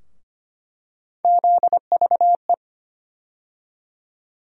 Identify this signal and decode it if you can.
Morse code, heard as a keyed tone, about 25 wpm, 700 Hz. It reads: ZVE
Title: USA Call Sign Postfixes - 3 Letters 25wpm